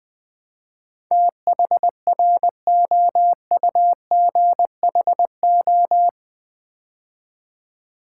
Morse code, heard as a keyed tone, about 20 wpm, 700 Hz. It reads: THROUGHO
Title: Build A Word - 4 Letters At A Time - 10-Letter Words - 20wpm